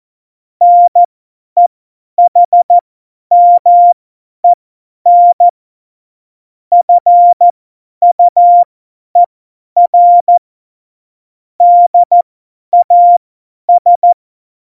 Morse code, heard as a keyed tone, about 14 wpm, spaced as for 10 wpm, 700 Hz. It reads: NEHMEN FUER DAS
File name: Morsekurs / Witze / Witz 11. September 2018, 50 F70